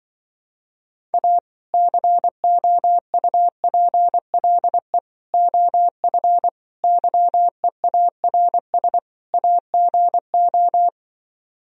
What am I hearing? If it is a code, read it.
Morse code, heard as a keyed tone, about 24 wpm, 700 Hz. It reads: A COUPLE OF YEARS AGO
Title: Speed-Racing - Sets of 5 Words 20wpm